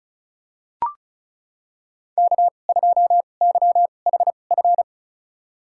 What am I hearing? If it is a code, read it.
Morse code, heard as a keyed tone, about 36 wpm, spaced as for 25 wpm, 700 Hz. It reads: K2YHF
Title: USA Call Signs - 5 Characters - 2x Character Spacing - 35wpm